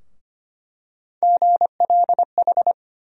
Morse code, heard as a keyed tone, about 25 wpm, 700 Hz. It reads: GLH
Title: USA Call Sign Postfixes - 3 Letters 25wpm